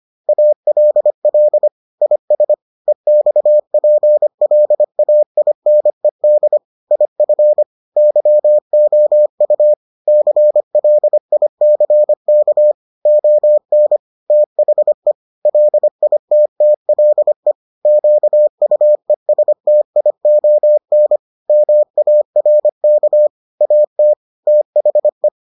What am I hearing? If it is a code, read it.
Morse code, heard as a keyed tone, about 25 wpm, 600 Hz. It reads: ALL IS EXPLAINED IF YOU CLICK ON THE LITTLE QUESTION MARK AT THE